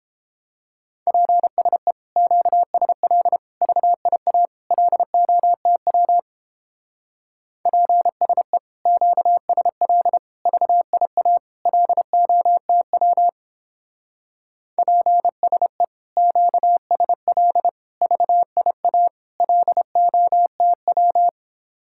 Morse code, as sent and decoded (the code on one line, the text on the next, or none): .--. ... / --.- ... .-.. / ...- .. .- / .-.. --- - .-- / .--. ... / --.- ... .-.. / ...- .. .- / .-.. --- - .-- / .--. ... / --.- ... .-.. / ...- .. .- / .-.. --- - .--
PSE QSL VIA LOTW PSE QSL VIA LOTW PSE QSL VIA LOTW